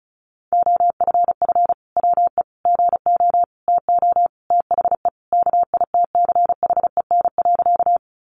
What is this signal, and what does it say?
Morse code, heard as a keyed tone, about 34 wpm, 700 Hz. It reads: OFF WE GO TO THE KITCHEN.
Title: Sentences from Top 1400 Words 35wpm